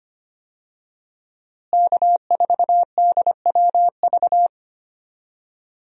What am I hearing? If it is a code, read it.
Morse code, heard as a keyed tone, about 25 wpm, 700 Hz. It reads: K4DWV
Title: USA Call Signs - 5 Characters - Repeated 2x - 25wpm